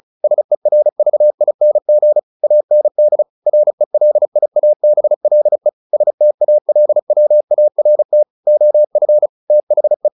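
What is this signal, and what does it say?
Morse code, heard as a keyed tone, about 35 wpm, 600 Hz. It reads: SERVING AND RELIABLE STALWART OF THE
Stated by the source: Morse practice at 35 wpm from CW med Gnister, sending Ken Follett's novel Never in English.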